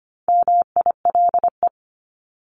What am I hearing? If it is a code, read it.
Morse code, heard as a keyed tone, about 25 wpm, 700 Hz. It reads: MILE
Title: ES:S Suffix - Top 100 Words with Base Words 25wpm